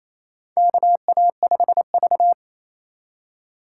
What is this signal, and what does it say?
Morse code, heard as a keyed tone, about 28 wpm, 700 Hz. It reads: KA5V
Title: USA Call Signs - 4 Characters - 3x Character Spacing - 28wpm